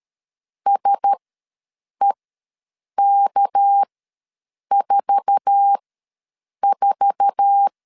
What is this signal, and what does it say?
Morse code, heard as a keyed tone, about 13 wpm, spaced as for 7 wpm, 800 Hz. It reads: SEK44
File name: LEZIONI / 8 YJ45 / 8 YJ4 ws 12wpm 3249